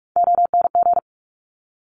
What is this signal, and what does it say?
Morse code, heard as a keyed tone, about 46 wpm, 690 Hz. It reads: ONG
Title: Top Three Letter Combinations in Words 45wpm